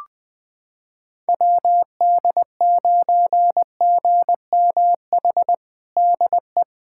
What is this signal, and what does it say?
Morse code, heard as a keyed tone, about 21 wpm, 700 Hz. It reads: WD9GMH DE
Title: USA Callsign DE Callsign 20wpm